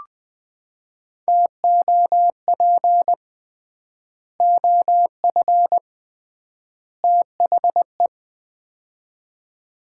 Morse code, heard as a keyed tone, about 21 wpm, 700 Hz. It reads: TOP OF THE
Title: 3x Word-Spacing - Sets of 3 Words - 20wpm